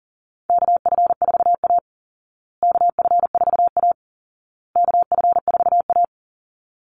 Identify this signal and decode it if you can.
Morse code, heard as a keyed tone, about 40 wpm, 700 Hz. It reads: KF4A KF4A KF4A
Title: Call Signs Repeated 2x - 40wpm